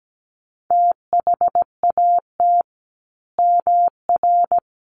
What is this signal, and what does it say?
Morse code, heard as a keyed tone, about 17 wpm, 700 Hz. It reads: THAT MR
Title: A Sound Of Thunder 17wpm